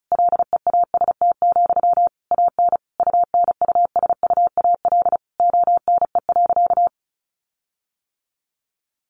Morse code, heard as a keyed tone, about 34 wpm, 700 Hz. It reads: LEAST, AN UNUSUAL ONE.
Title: Sentences from Top 1400 Words 35wpm